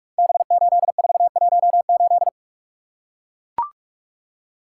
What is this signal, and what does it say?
Morse code, heard as a keyed tone, about 44 wpm, 700 Hz. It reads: B8418
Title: Code Groups - 5 Letter:Numbers - 45wpm